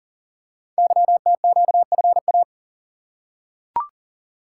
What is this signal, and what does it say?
Morse code, heard as a keyed tone, about 39 wpm, 700 Hz. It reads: YTQFA
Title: Code Groups - 5 Letters - 40wpm